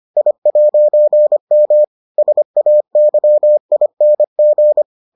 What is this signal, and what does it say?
Morse code, heard as a keyed tone, about 25 wpm, 600 Hz. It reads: I'M SAYING